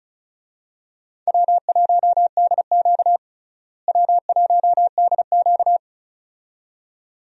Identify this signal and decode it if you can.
Morse code, heard as a keyed tone, about 36 wpm, 700 Hz. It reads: W1DQ W1DQ
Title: USA Call Signs - 4 Characters - Repeated 2x - 35wpm